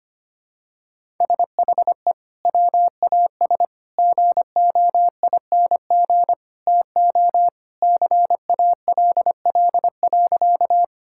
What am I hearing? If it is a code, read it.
Morse code, heard as a keyed tone, about 25 wpm, 700 Hz. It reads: SHE WAS GOING TO CALL.
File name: practice / 25wpm / Sentences with 3-Word Phrases - 25wpm